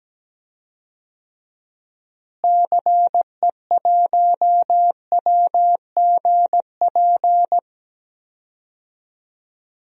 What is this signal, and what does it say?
Morse code, heard as a keyed tone, about 17 wpm, 700 Hz. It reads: CE1WGP
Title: DXCC-100 Call Signs - 6 Characters - Repeated 3x - 17wpm